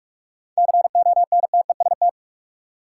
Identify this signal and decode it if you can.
Morse code, heard as a keyed tone, about 46 wpm, 700 Hz. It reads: CONTEST
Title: Seven Letter Words - 45wpm